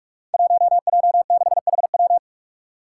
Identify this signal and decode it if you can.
Morse code, heard as a keyed tone, about 44 wpm, 700 Hz. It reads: 1J6HW
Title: Five Challenging Letter-Number Combinations 45wpm